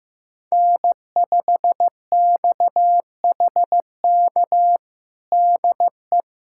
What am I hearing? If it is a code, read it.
Morse code, heard as a keyed tone, about 15 wpm, 700 Hz. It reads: N5XHK DE
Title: USA Callsign DE Callsign - 5 Characters - 15wpm